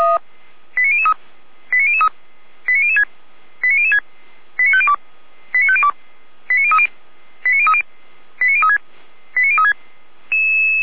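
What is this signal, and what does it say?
Five-tone selective call (ZVEI), heard as quick runs of five tones, tones of about 70 ms. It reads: 89003 89003 89006 89006 89502 89502 89030 89030 89036 89036
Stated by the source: ZVEI five-tone selective call, code 89001, received with an SDR, from feuerwehr-orlamuende.org (fire brigade Orlamünde).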